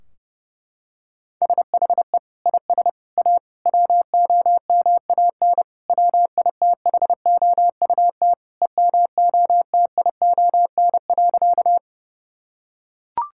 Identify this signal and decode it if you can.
Morse code, heard as a keyed tone, about 29 wpm, 700 Hz. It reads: SHE IS A WOMAN WITHOUT EMOTION.
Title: Sentences from Top 1600 Words 30wpm